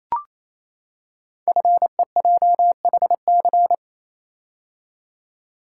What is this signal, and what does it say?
Morse code, heard as a keyed tone, about 29 wpm, 700 Hz. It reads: FEJHC
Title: Code Groups - 5 Letters - 28wpm